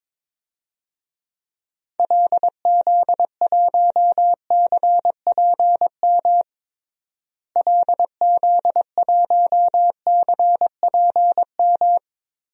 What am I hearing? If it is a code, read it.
Morse code, heard as a keyed tone, about 22 wpm, 700 Hz. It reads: LZ1CPM LZ1CPM
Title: DXCC-100 Call Signs - 6 Characters - Repeated 3x - 22wpm